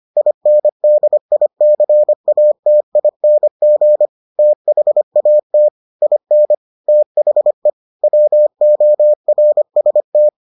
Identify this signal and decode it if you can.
Morse code, heard as a keyed tone, about 25 wpm, 600 Hz. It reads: INDICATING THAT IN THE WORST